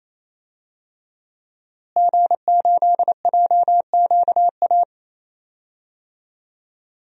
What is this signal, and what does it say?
Morse code, heard as a keyed tone, about 28 wpm, 700 Hz. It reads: G8JQA
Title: DXCC-100 Call Signs - 5 Characters - Repeated 3x - 28wpm